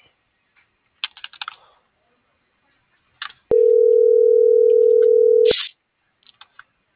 Ringback tone (North America), 440 + 480 Hz.